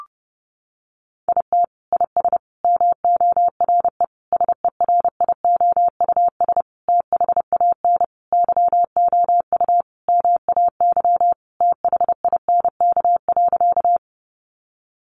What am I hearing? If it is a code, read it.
Morse code, heard as a keyed tone, about 28 wpm, 700 Hz. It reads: IT IS MORE SERIOUS THAN YOU MAY THINK.
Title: Sentences from Top 1100 Words 30wpm